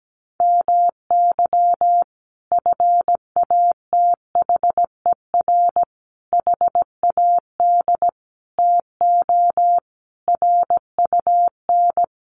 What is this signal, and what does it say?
Morse code, heard as a keyed tone, about 17 wpm, 700 Hz. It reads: MY FATHER HAD TO RUN
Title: Sentences from Top 1100 Words 17wpm